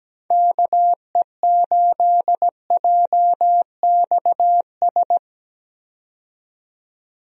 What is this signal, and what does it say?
Morse code, heard as a keyed tone, about 17 wpm, 700 Hz. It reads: KE8JXS
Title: USA Call Signs - 6 Characters - 17wpm